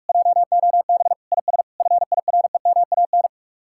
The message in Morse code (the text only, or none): JOB IS FIREMAN